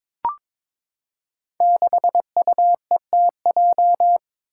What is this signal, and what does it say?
Morse code, heard as a keyed tone, about 22 wpm, 700 Hz. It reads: 6UETJ
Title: Five Challenging Letter-Number Combinations 22wpm